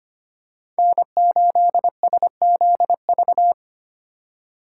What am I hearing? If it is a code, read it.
Morse code, heard as a keyed tone, about 25 wpm, 700 Hz. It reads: N8SZV
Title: USA Call Signs - 5 Characters - 4x Character Spacing - 25wpm